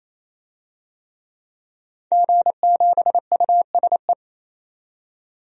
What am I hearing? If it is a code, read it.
Morse code, heard as a keyed tone, about 27 wpm, 700 Hz. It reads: G7USE
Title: DXCC-100 Call Signs - 5 Characters - 28wpm